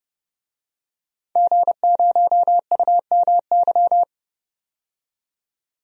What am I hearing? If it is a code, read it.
Morse code, heard as a keyed tone, about 31 wpm, 700 Hz. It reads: G0UMY
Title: DXCC-100 Call Signs - 5 Characters - 30wpm